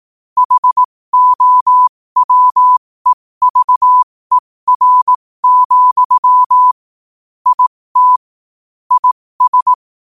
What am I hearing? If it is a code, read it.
Morse code, heard as a keyed tone, about 18 wpm, spaced as for 15 wpm, 1000 Hz. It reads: HOWEVER, IT IS
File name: qotdfiles / 15wpm